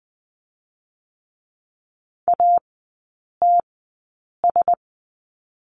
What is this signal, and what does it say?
Morse code, heard as a keyed tone, about 20 wpm, spaced as for 8 wpm, 700 Hz. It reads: ATS
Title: 121 Lesson 20e - Number 3 - Three Characters 20wpm